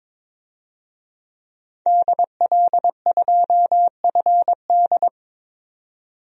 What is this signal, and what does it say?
Morse code, heard as a keyed tone, about 22 wpm, 700 Hz. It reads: DL2FD
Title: DXCC-100 Call Signs - 5 Characters - 22wpm